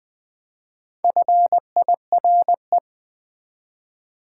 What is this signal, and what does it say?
Morse code, heard as a keyed tone, about 20 wpm, 700 Hz. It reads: FIRE
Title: Build A Word - 2 Letters At A Time - 4-Letter Words - 20wpm